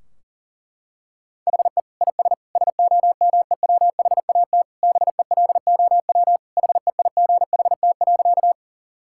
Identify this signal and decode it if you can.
Morse code, heard as a keyed tone, about 40 wpm, 700 Hz. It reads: HE IS SOMEWHAT BELOW HEIGHT.
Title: Sentences from Top 1600 Words 40wpm